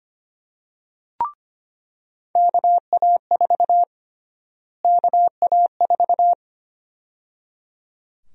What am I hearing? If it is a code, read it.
Morse code, heard as a keyed tone, about 25 wpm, 700 Hz. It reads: KA4 KA4
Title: USA Call Sign Prefixes - Repeated 2x - 25wpm